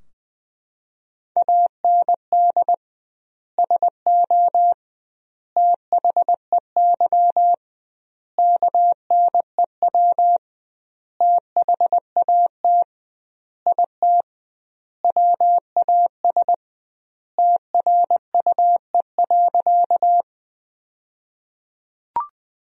Morse code, heard as a keyed tone, about 21 wpm, 700 Hz. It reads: AND SO THEY KNEW THAT IT WAS TRUE.
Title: 2x Word-Spacing - Sentences from Top 400 Words - 20wpm